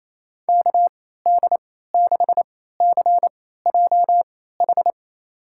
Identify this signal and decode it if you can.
Morse code, heard as a keyed tone, about 28 wpm, spaced as for 16 wpm, 700 Hz. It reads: KD6CJH
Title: USA Call Signs - 6 Characters - 3x Character Spacing - 28wpm